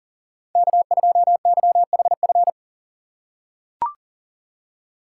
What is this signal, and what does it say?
Morse code, heard as a keyed tone, about 40 wpm, 700 Hz. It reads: K2YHF
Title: USA Call Signs - 5 Characters - 3x Character Spacing - 40wpm